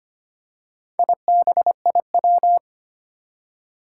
Morse code, heard as a keyed tone, about 25 wpm, 700 Hz. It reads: IBIW
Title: Four Challenging Letter-Number Combinations 25wpm